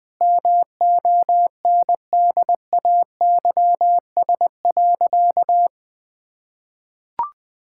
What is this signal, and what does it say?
Morse code, heard as a keyed tone, about 21 wpm, 700 Hz. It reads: MONDAYS.
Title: Easy Sentences with Hints - 20wpm